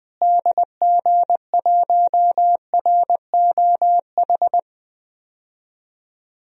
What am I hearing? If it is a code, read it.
Morse code, heard as a keyed tone, about 20 wpm, 700 Hz. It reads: DG1ROH
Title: DXCC-100 Call Signs - 6 Characters - Repeated 2x - 20wpm